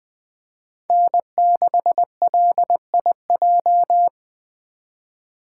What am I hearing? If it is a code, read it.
Morse code, heard as a keyed tone, about 20 wpm, 700 Hz. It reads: N6LIJ
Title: Speed Racing - USA 1x3 Call Signs 17wpm